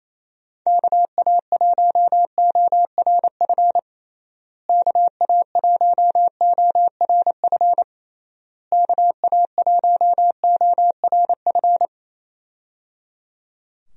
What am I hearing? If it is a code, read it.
Morse code, heard as a keyed tone, about 28 wpm, 700 Hz. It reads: KA1ORF KA1ORF KA1ORF
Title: USA Call Signs - 6 Characters - Repeated 3x - 28wpm